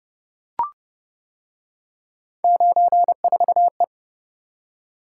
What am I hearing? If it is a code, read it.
Morse code, heard as a keyed tone, about 28 wpm, 700 Hz. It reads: 94E